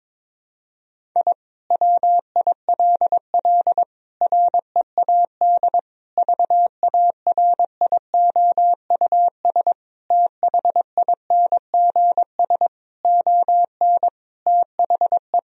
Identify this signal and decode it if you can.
Morse code, heard as a keyed tone, about 22 wpm, 700 Hz. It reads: I WILL READ VARIOUS THINGS ON THE
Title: Sentences from Top 1000 Words - Review Part 4 - 22wpm